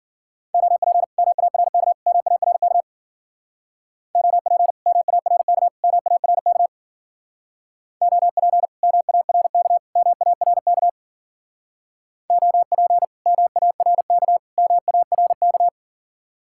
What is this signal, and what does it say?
Morse code, heard as a keyed tone, about 48 wpm, 700 Hz. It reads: OP MARK MARK OP MARK MARK OP MARK MARK OP MARK MARK
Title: Speed-Racing Common QSO Phrases Encore - 40wpm